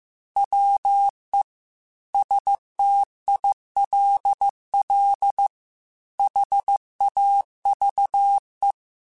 Morse code, heard as a keyed tone, about 15 wpm, 800 Hz. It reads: WE STILL HAVE
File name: QSOs / QSOs-mp3s / WA7QOX 1@13